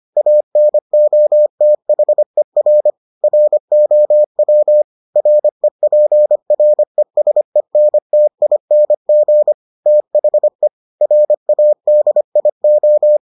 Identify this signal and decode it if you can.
Morse code, heard as a keyed tone, about 25 wpm, 600 Hz. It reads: ANOTHER ROW REPRESENTING THE RADIO